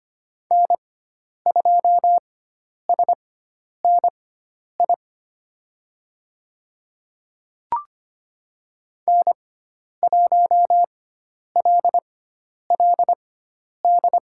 Morse code, heard as a keyed tone, about 25 wpm, spaced as for 10 wpm, 700 Hz. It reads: N2SNI N1LLD
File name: Lesson 16g - Letter c - Call Signs 25wpm